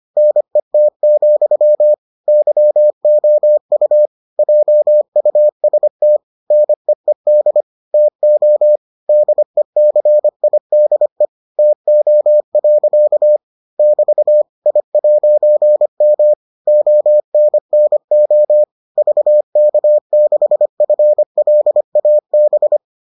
Morse code, heard as a keyed tone, about 25 wpm, 600 Hz. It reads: NET, YOU JUST NEED TO DECIDE TO. <BT> I'M ONNO VK6FLAB